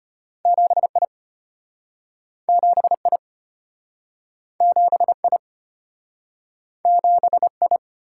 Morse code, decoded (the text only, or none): 7I 7I 7I 7I